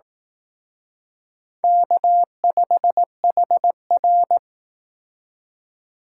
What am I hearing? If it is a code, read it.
Morse code, heard as a keyed tone, about 19 wpm, 700 Hz. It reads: K5HR
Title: Speed Racing - USA 1x2 Call Signs 15wpm